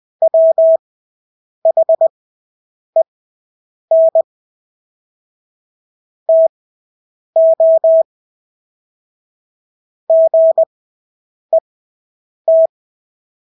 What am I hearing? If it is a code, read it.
Morse code, heard as a keyed tone, about 20 wpm, spaced as for 8 wpm, 650 Hz. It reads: WHEN TO GET
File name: Engelske ord 20/8 wpm